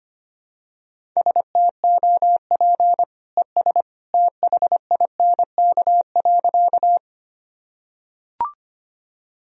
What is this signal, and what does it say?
Morse code, heard as a keyed tone, about 25 wpm, 700 Hz. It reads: STOP ES THINK.
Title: Sentences from Top 300 Words and Abbreviations 25wpm